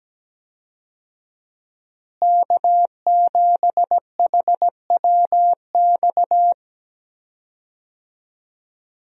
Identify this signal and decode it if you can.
Morse code, heard as a keyed tone, about 17 wpm, 700 Hz. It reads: K7HWX
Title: USA Call Signs - 5 Characters - Repeated 3x - 17wpm